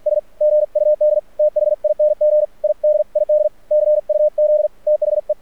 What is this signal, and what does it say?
Morse code, heard as a keyed tone, about 48 wpm, spaced as for 41 wpm, 600 Hz, with radio noise. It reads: SGUD THING IDID CUZ THE
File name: archived / CRSnet